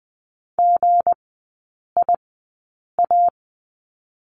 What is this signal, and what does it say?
Morse code, heard as a keyed tone, about 20 wpm, spaced as for 8 wpm, 700 Hz. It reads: GIA
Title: 178 Lesson 29e - Slash - Three Characters 20wpm